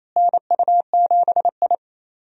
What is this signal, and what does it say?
Morse code, heard as a keyed tone, about 28 wpm, 700 Hz. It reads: NU7I
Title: USA Call Signs - 4 Characters - Repeated 2x - 28wpm